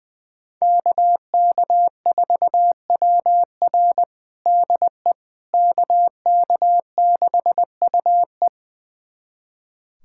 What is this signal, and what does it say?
Morse code, heard as a keyed tone, about 20 wpm, 700 Hz. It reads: KK4WR DE KK6UE
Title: USA Callsign DE Callsign - 5 Characters - 20wpm